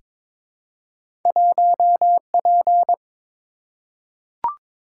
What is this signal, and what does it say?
Morse code, heard as a keyed tone, about 22 wpm, 690 Hz. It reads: 1P
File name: practice / 22wpm / Two Letter-Number Combinations - 22wpm